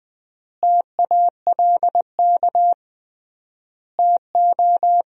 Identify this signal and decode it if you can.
Morse code, heard as a keyed tone, about 20 wpm, 700 Hz. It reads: TALK TO
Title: 3x Word-Spacing - Sentences from Top 400 Words - 20wpm